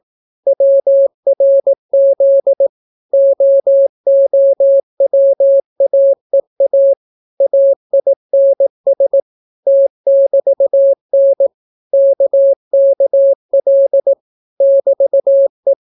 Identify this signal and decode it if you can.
Morse code, heard as a keyed tone, about 18 wpm, 550 Hz. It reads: WRZ OOWAEA AINS T<BT>N KKL <BT>E